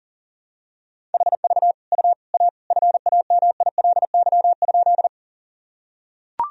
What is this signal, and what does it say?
Morse code, heard as a keyed tone, about 40 wpm, 700 Hz. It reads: HV U A FAMILY?
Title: Sentences from Top 400 Words and Abbreviations 40wpm